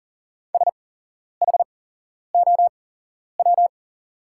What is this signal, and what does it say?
Morse code, heard as a keyed tone, about 39 wpm, spaced as for 11 wpm, 700 Hz. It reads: SHOW
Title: 8x Character Spacing - Top 200 Words - 40wpm